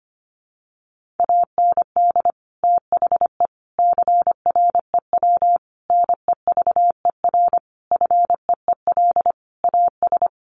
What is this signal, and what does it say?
Morse code, heard as a keyed tone, about 25 wpm, 700 Hz. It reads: AND THE CREW NEVER FEEL AS
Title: Sentences from Top 1500 Words 25wpm